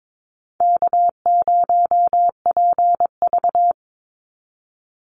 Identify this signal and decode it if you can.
Morse code, heard as a keyed tone, about 22 wpm, 700 Hz. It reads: K0PV
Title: Call Signs Repeated 3x - 22wpm